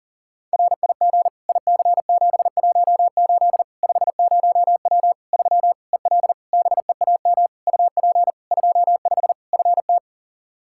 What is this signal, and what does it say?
Morse code, heard as a keyed tone, about 40 wpm, 700 Hz. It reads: RIG IC718 50W 3 EL BEAM UP 25 FT
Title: Practice QSO - Exchange 2 - Info - 40wpm